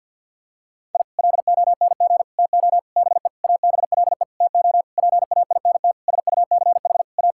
Morse code, heard as a keyed tone, about 52 wpm, 700 Hz. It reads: I LONG TO BE ABLE TO PAINT SUCH A
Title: Sentences from Top 1000 Words - Review Part 6 - 50wpm